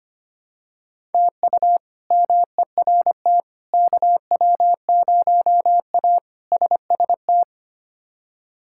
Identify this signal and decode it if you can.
Morse code, heard as a keyed tone, about 25 wpm, 700 Hz. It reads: TU MERT KW0A SST
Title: K1USN SST QSO Practice - 25wpm